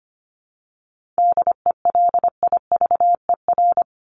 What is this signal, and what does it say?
Morse code, heard as a keyed tone, about 25 wpm, 700 Hz. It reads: DELIVER
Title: Top 1751-2000 Words 25wpm